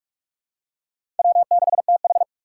Morse code, heard as a keyed tone, about 40 wpm, 700 Hz. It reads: W6TH